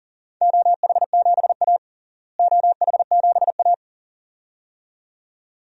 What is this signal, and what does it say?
Morse code, heard as a keyed tone, about 39 wpm, 700 Hz. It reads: OH7A OH7A
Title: DXCC-100 Call Signs - 4 Characters - Repeated 2x - 40wpm